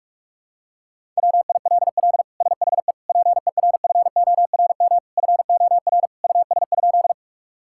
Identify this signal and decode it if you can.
Morse code, heard as a keyed tone, about 42 wpm, 700 Hz. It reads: WILL SHE PERFORM FOR US?
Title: Sentences from Top 1000 Words - Review Part 5 - 45wpm